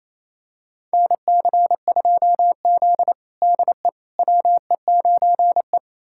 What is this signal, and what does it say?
Morse code, heard as a keyed tone, about 28 wpm, 700 Hz. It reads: NC2Z DE WE9E
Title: USA Callsign DE Callsign - 4 Characters - 28wpm